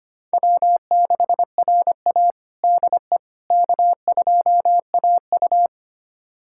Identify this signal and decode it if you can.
Morse code, heard as a keyed tone, about 25 wpm, 700 Hz. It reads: W6RA DE K2AU
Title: USA Callsign DE Callsign - 4 Characters - 25wpm